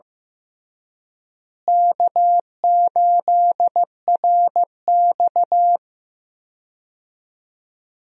Morse code, heard as a keyed tone, about 15 wpm, 700 Hz. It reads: K8RX
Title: USA Call Signs - 4 Characters - Repeated 2x - 15wpm